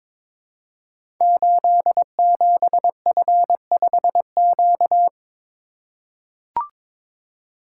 Morse code, heard as a keyed tone, about 21 wpm, 700 Hz. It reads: 87F5Q
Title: Code Groups - 5 Letter:Numbers - 22wpm